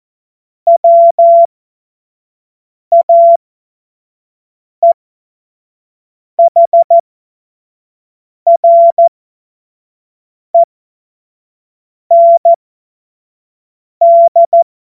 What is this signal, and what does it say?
Morse code, heard as a keyed tone, about 14 wpm, spaced as for 5 wpm, 680 Hz. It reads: WAEHREND
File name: Morsekurs / Witze / Witz 05 Dezember 2023 25 F70